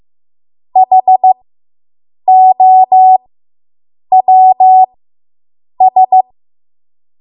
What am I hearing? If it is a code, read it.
Morse code, heard as a keyed tone, about 15 wpm, spaced as for 7 wpm, 750 Hz. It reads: HOWS